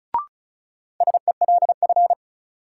Morse code, heard as a keyed tone, about 36 wpm, 700 Hz. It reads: SELF
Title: Sentences from Top 1000 Words - Review Part 1 - New Word Drill - 35wpm